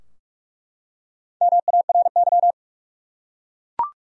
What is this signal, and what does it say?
Morse code, heard as a keyed tone, about 46 wpm, 700 Hz. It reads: MARY
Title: USA Common Names - 45wpm